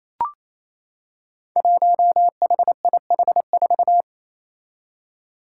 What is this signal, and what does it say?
Morse code, heard as a keyed tone, about 28 wpm, 700 Hz. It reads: 1HIH4
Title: Five Challenging Letter-Number Combinations 28wpm